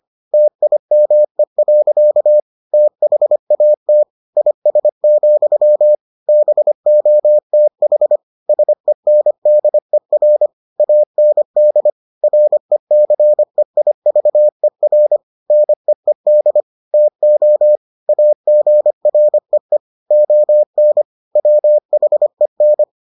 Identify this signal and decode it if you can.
Morse code, heard as a keyed tone, about 25 wpm, 600 Hz. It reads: TIME. THAT IS, BOTH SENDER AND RECEIVER NEED TO AGREE ON WHEN